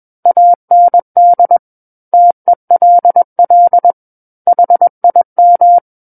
Morse code, heard as a keyed tone, about 21 wpm, 700 Hz. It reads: AND TELL HIM
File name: Book One / 12-Chapter 12 - 21 WPM